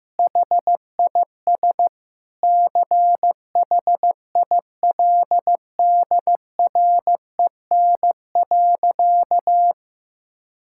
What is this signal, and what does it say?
Morse code, heard as a keyed tone, about 15 wpm, 700 Hz. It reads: HIS CHILDREN.
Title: Sentences from Top 1000 Words - Review Part 5 - 15wpm